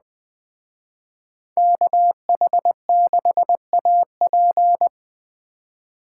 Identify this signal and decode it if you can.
Morse code, heard as a keyed tone, about 20 wpm, 700 Hz. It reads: KH6AP
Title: Speed Racing - USA 2x2 Call Signs 17wpm